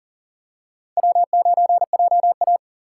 Morse code, heard as a keyed tone, about 40 wpm, 700 Hz. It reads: W9JA
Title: USA Call Signs - 4x Character Spacing - 40wpm